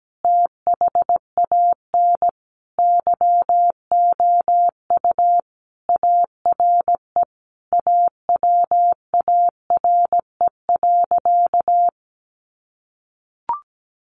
Morse code, heard as a keyed tone, about 17 wpm, 700 Hz. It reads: THAN YOU ARE AWARE.
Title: Sentences from Top 1300 Words 17wpm